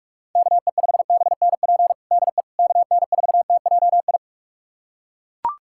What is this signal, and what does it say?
Morse code, heard as a keyed tone, about 42 wpm, 700 Hz. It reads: KE5BNP DE KN4TJI
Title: USA Callsign DE Callsign 45wpm